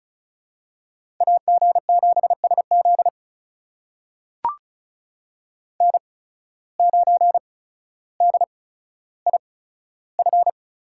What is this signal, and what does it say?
Morse code, heard as a keyed tone, about 34 wpm, spaced as for 15 wpm, 700 Hz. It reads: AG7SZ N9DIF
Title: USA Call Signs - 5 Characters - 8x Character Spacing - 35wpm